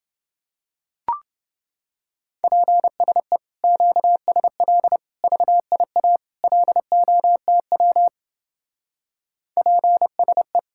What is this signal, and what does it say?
Morse code, heard as a keyed tone, about 29 wpm, 700 Hz. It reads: PSE QSL VIA LOTW PSE